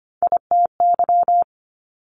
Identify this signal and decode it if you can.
Morse code, heard as a keyed tone, about 25 wpm, 700 Hz. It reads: ITY
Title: Top Three Letter Combinations in Words 25wpm